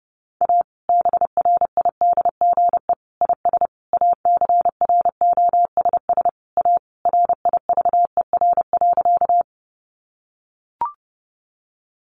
Morse code, heard as a keyed tone, about 30 wpm, 700 Hz. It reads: A BRIDGE IS ACROSS A RIVER.
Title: Sentences from Top 1200 Words 30wpm